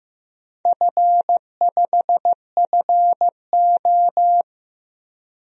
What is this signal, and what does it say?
Morse code, heard as a keyed tone, about 15 wpm, 700 Hz. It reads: F5FO
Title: DXCC-100 Call Signs - 4 Characters - 15wpm